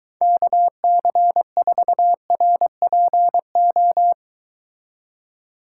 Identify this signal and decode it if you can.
Morse code, heard as a keyed tone, about 22 wpm, 700 Hz. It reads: KC4RPO